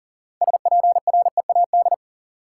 Morse code, heard as a keyed tone, about 40 wpm, 700 Hz. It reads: SPREAD
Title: Sentences from Top 1200 Words - New Word Drill - 40wpm